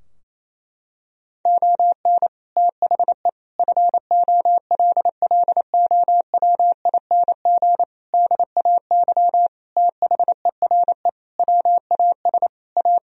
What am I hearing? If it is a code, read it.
Morse code, heard as a keyed tone, about 28 wpm, 700 Hz. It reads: ON THE FOLLOWING DAY THERE WAS A
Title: Sentences from Top 1000 Words - Review Part 4 - 28wpm